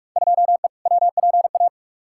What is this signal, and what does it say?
Morse code, heard as a keyed tone, about 43 wpm, 700 Hz. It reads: 2E WPA
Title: Field Day Exchanges - 45wpm